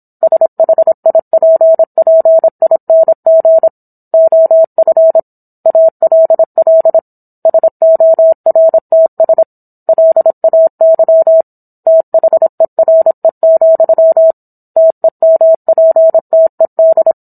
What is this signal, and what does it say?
Morse code, heard as a keyed tone, about 27 wpm, 650 Hz. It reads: SHIPPING OF ALL SORTS LAY THERE, TEMPTED